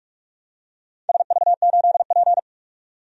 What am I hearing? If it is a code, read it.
Morse code, heard as a keyed tone, about 44 wpm, 700 Hz. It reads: SV8P
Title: DXCC-100 Call Signs - 4 Characters - 45wpm